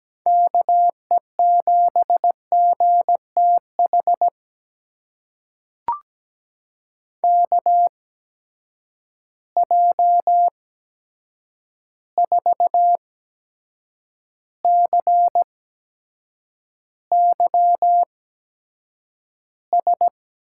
Morse code, heard as a keyed tone, about 17 wpm, spaced as for 7 wpm, 700 Hz. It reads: KE7GTH KJ4CYS
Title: USA Call Signs - 6 Characters - 8x Character Spacing - 17wpm